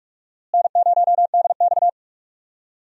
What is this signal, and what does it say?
Morse code, heard as a keyed tone, about 45 wpm, 700 Hz. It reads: N0DX
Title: USA Call Signs - 4 Characters - Repeated 2x - 45wpm